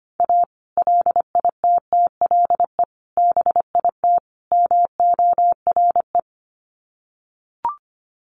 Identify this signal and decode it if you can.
Morse code, heard as a keyed tone, about 25 wpm, 700 Hz. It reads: A LITTLE BIT MORE
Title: Sets of 4 Words 25wpm